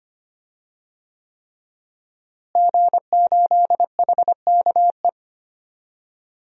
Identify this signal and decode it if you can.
Morse code, heard as a keyed tone, about 25 wpm, 700 Hz. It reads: G8HKE
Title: DXCC-100 Call Signs - 5 Characters - Repeated 3x - 25wpm